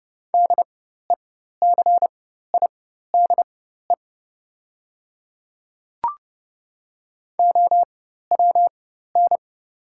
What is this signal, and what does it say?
Morse code, heard as a keyed tone, about 30 wpm, spaced as for 14 wpm, 700 Hz. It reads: DECIDE OWN
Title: 4x Character Spacing - Top 500 Words - 30wpm